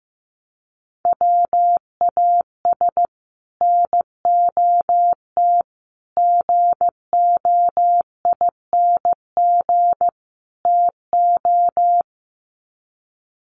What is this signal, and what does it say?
Morse code, heard as a keyed tone, about 15 wpm, 700 Hz. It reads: WAS NOT GOING TO
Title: Sets of 4 Words 15wpm